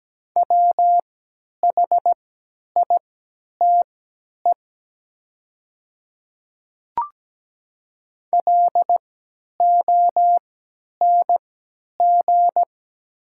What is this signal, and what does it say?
Morse code, heard as a keyed tone, about 17 wpm, spaced as for 10 wpm, 700 Hz. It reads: WHITE LONG
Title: 3x Character Spacing - Top 300 Words - 17wpm